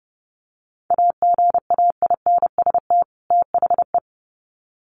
Morse code, heard as a keyed tone, about 28 wpm, 700 Hz. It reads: AGAINST THE
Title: Sets of 2 Words 30wpm